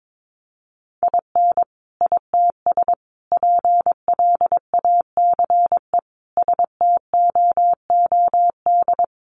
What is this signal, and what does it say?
Morse code, heard as a keyed tone, about 22 wpm, 700 Hz. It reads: IN ITS PLACE STOOD
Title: Sentences from Top 1400 Words 22wpm